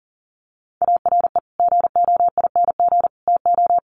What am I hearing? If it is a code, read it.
Morse code, heard as a keyed tone, about 40 wpm, 700 Hz. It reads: ARE GOING TO
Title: Sets of 3 Words 40wpm